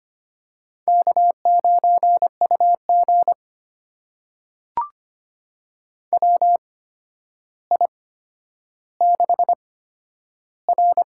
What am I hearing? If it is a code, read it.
Morse code, heard as a keyed tone, about 25 wpm, spaced as for 11 wpm, 700 Hz. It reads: K9UG WI6R
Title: USA Call Signs - 4 Characters - 8x Character Spacing - 25wpm